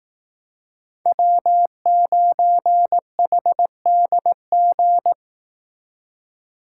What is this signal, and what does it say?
Morse code, heard as a keyed tone, about 18 wpm, 700 Hz. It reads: W9HDG
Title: Speed Racing - USA 1x3 Call Signs 15wpm